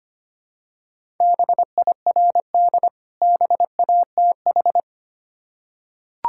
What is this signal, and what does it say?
Morse code, heard as a keyed tone, about 25 wpm, 700 Hz. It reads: BIRD BATH
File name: practice / 25wpm / Difficult Words with Hints - 25wpm